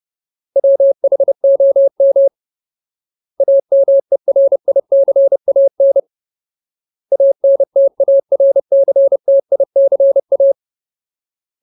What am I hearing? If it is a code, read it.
Morse code, heard as a keyed tone, about 30 wpm, 550 Hz. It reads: WHOM AMERICAN ANTARCTICA
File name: Ultimate Word List / 30WPM Ultimate